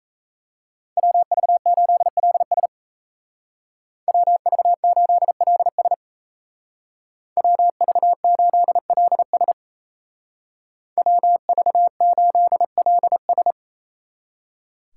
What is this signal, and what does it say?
Morse code, heard as a keyed tone, about 33 wpm, 700 Hz. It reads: WV8LS WV8LS WV8LS WV8LS
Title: Speed Racing - USA Call Signs 28wpm